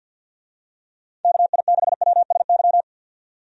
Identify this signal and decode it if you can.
Morse code, heard as a keyed tone, about 46 wpm, 700 Hz. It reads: KI6WSY
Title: USA Call Signs - 6 Characters - 50wpm